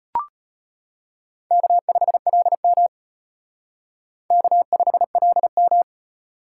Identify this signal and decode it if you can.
Morse code, heard as a keyed tone, about 31 wpm, 700 Hz. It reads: K5LM K5LM
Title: Speed Racing - USA 1x2 Call Signs 25wpm